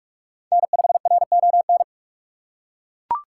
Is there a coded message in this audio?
Yes: Morse code from 0.5 to 1.8 s, heard as a keyed tone, about 40 wpm, 700 Hz. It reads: N5RON